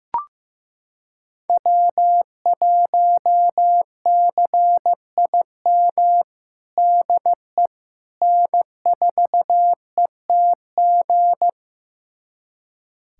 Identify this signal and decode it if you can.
Morse code, heard as a keyed tone, about 15 wpm, 700 Hz. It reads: W1CIM DE N4ETG
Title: USA Callsign DE Callsign - 5 Characters - 15wpm